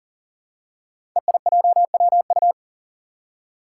Morse code, heard as a keyed tone, about 40 wpm, 700 Hz. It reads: EIJWU
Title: Five Challenging Letter-Number Combinations 40wpm